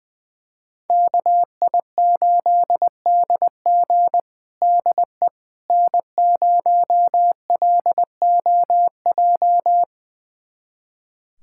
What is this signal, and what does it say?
Morse code, heard as a keyed tone, about 21 wpm, 700 Hz. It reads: KI8DG DE N0LOJ
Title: USA Callsign DE Callsign - 5 Characters - 20wpm